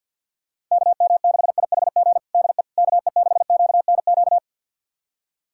Morse code, heard as a keyed tone, about 52 wpm, 700 Hz. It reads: KM6IHG DE KE6QNQ
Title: USA Callsign DE Callsign - 6 Characters - 50wpm